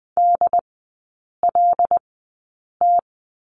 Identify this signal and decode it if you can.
Morse code, heard as a keyed tone, about 20 wpm, spaced as for 8 wpm, 690 Hz. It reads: DLT
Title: 205 Lesson 33e - Letter j - Three Characters 20wpm